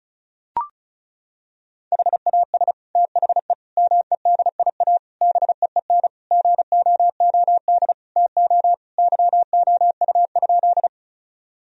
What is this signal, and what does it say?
Morse code, heard as a keyed tone, about 34 wpm, 700 Hz. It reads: HAS THE MEDIA BEEN GOOD TO YOU?
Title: Sentences from Top 1000 Words - Review Part 4 - 35wpm